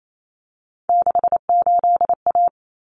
Morse code, heard as a keyed tone, about 28 wpm, 700 Hz. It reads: 68A
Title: Common Cut Numbers - 3-Digit 28wpm